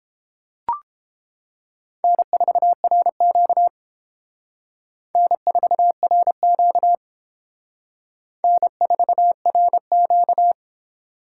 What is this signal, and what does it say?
Morse code, heard as a keyed tone, about 29 wpm, 700 Hz. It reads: N4RQ N4RQ N4RQ